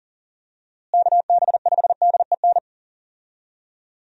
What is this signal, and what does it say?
Morse code, heard as a keyed tone, about 40 wpm, 700 Hz. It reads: KB5DEN
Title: USA Call Signs - 6 Characters - Repeated 2x - 40wpm